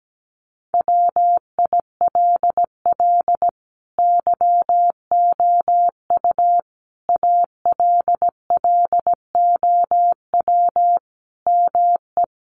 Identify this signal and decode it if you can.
Morse code, heard as a keyed tone, about 17 wpm, 700 Hz. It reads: WILL YOU ALLOW ME
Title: Sentences from Top 1200 Words 17wpm